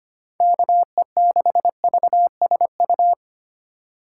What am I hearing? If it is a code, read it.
Morse code, heard as a keyed tone, about 25 wpm, 700 Hz. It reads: KE6VSU
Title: USA Call Signs - 6 Characters - 4x Character Spacing - 25wpm